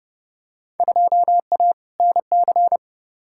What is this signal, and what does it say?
Morse code, heard as a keyed tone, about 32 wpm, 700 Hz. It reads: 2A NC
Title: Field Day Exchanges - 30wpm